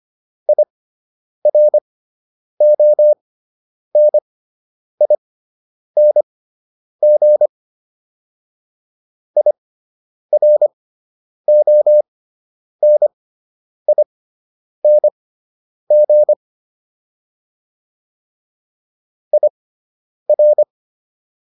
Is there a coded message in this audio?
Yes: Morse code from 0.5 to 20.6 s, heard as a keyed tone, about 25 wpm, spaced as for 9 wpm, 600 Hz. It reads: IRONING IRONING IR